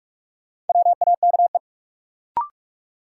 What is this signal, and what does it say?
Morse code, heard as a keyed tone, about 46 wpm, 700 Hz. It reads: WAKE
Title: Sets of 4 Words - New Word Drill - 45wpm